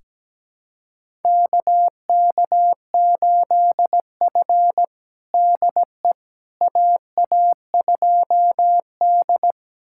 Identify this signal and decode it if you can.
Morse code, heard as a keyed tone, about 17 wpm, 700 Hz. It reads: KK8F DE AA2D
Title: USA Callsign DE Callsign - 4 Characters - 17wpm